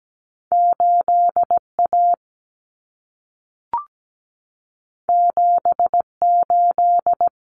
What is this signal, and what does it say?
Morse code, heard as a keyed tone, about 17 wpm, 700 Hz. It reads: 8A 78
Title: Common Cut Numbers - 2-Digit 17wpm